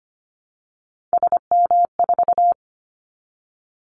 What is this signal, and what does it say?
Morse code, heard as a keyed tone, about 25 wpm, 700 Hz. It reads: SM4
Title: Callsign Prefixes for Top 25 DXCC - 25wpm